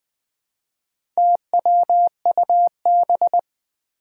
Morse code, heard as a keyed tone, about 20 wpm, 700 Hz. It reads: TWUB
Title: Four Challenging Letter-Number Combinations 20wpm